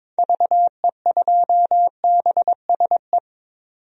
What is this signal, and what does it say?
Morse code, heard as a keyed tone, about 22 wpm, 700 Hz. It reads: VE2BSE
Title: DXCC-100 Call Signs - 6 Characters - 22wpm